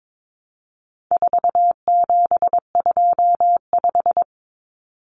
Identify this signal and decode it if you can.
Morse code, heard as a keyed tone, about 22 wpm, 700 Hz. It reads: 4725
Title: Numbers - 4-Digit 22wpm